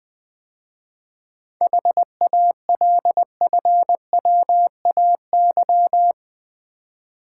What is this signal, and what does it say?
Morse code, heard as a keyed tone, about 20 wpm, 700 Hz. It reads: HALFWAY
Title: 7 Letter Words - First 3 Letters Given - 20wpm